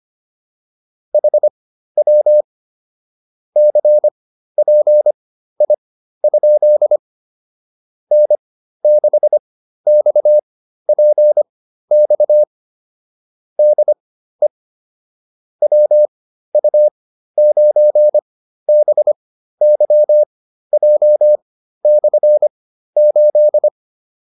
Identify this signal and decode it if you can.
Morse code, heard as a keyed tone, about 25 wpm, spaced as for 13 wpm, 600 Hz. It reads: HW CPI? N6XPX DE WU9BYJ/8